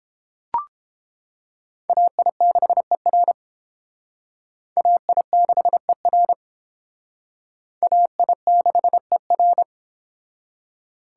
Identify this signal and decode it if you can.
Morse code, heard as a keyed tone, about 28 wpm, 690 Hz. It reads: AI6ER AI6ER AI6ER